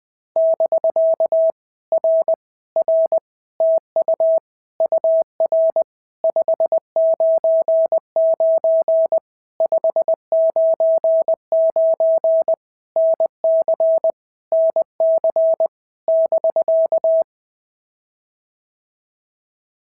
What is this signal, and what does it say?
Morse code, heard as a keyed tone, about 20 wpm, 650 Hz. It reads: <BK> R R TU UR 599 599 NC NC <BK>
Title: POTA QSO Practice - 20wpm